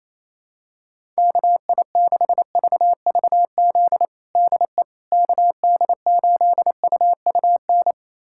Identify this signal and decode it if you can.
Morse code, heard as a keyed tone, about 28 wpm, 700 Hz. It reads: KI6VVZ DE KD8UUN
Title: USA Callsign DE Callsign - 6 Characters - 28wpm